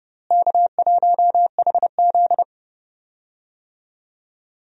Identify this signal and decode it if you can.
Morse code, heard as a keyed tone, about 31 wpm, 700 Hz. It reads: K1HZ
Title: USA Call Signs - 4 Characters - Repeated 2x - 30wpm